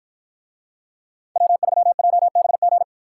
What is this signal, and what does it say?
Morse code, heard as a keyed tone, about 51 wpm, 700 Hz. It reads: W3JBG